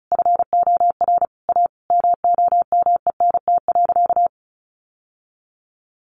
Morse code, heard as a keyed tone, about 34 wpm, 700 Hz. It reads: FOR A MOMENT.
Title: Sentences from Top 1000 Words 35wpm